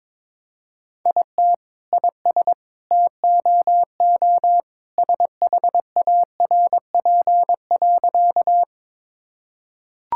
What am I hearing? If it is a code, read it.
Morse code, heard as a keyed tone, about 22 wpm, 700 Hz. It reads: IT IS TOO SHARP.
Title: Sentences from Top 1600 Words 22wpm